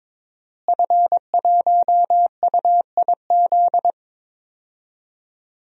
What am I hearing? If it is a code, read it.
Morse code, heard as a keyed tone, about 22 wpm, 700 Hz. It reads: F1UIZ
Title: DXCC-100 Call Signs - 5 Characters - Repeated 2x - 22wpm